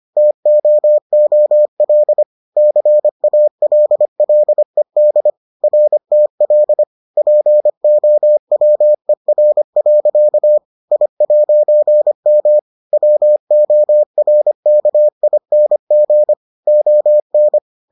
Morse code, heard as a keyed tone, about 25 wpm, 600 Hz. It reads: TOOL CALLED RTL POWER. I'M WORKING ON